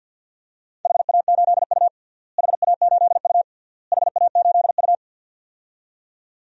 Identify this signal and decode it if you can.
Morse code, heard as a keyed tone, about 48 wpm, 700 Hz. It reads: HA8U HA8U HA8U
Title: DXCC-100 Call Signs - 4 Characters - Repeated 3x - 50wpm